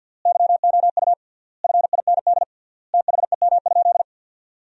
Morse code, heard as a keyed tone, about 50 wpm, 700 Hz. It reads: YOU FIND THEM?
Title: 3x Word-Spacing - Sentences from Top 200 Words - 50wpm